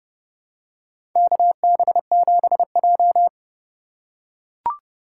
Morse code, heard as a keyed tone, about 30 wpm, 700 Hz. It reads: KB7J
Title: USA Call Signs - 4x Character Spacing - 30wpm